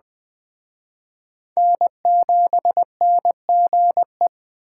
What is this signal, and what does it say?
Morse code, heard as a keyed tone, about 23 wpm, 700 Hz. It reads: N7NGE